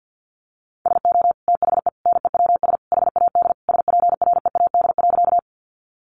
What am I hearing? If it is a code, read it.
Morse code, heard as a keyed tone, about 50 wpm, 700 Hz. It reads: SO THE NEWS HAD SPREAD.
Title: Sentences from Top 1200 Words 50wpm